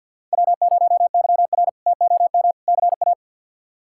Morse code, heard as a keyed tone, about 51 wpm, 700 Hz. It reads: W0YR TOM CA
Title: K1USN SST QSO Practice - 50wpm